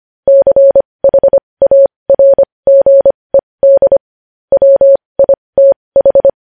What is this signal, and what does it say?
Morse code, heard as a keyed tone, about 25 wpm, spaced as for 20 wpm, 550 Hz. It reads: CHARGED WITH
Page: news-headlines-in-morse-code-at-20-wpm